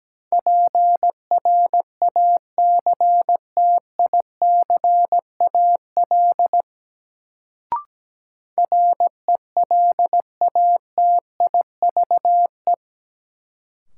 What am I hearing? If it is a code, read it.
Morse code, heard as a keyed tone, about 17 wpm, 700 Hz. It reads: PRACTICAL RELATIVE
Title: Sentences from Top 1400 Words - New Word Drill - 17wpm